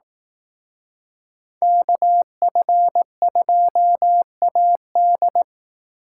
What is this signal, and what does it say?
Morse code, heard as a keyed tone, about 18 wpm, 700 Hz. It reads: KF2AD
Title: Speed Racing - USA 2x2 Call Signs 15wpm